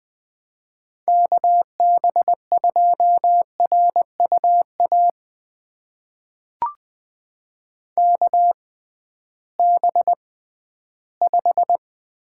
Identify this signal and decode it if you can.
Morse code, heard as a keyed tone, about 20 wpm, spaced as for 13 wpm, 700 Hz. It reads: KB2RUA KB5
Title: USA Call Signs - 6 Characters - 6x Character Spacing - 20wpm